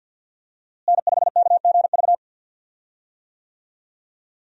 Morse code, heard as a keyed tone, about 52 wpm, 700 Hz. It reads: N5KGV
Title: USA Call Signs - 5 Characters - Repeated 2x - 50wpm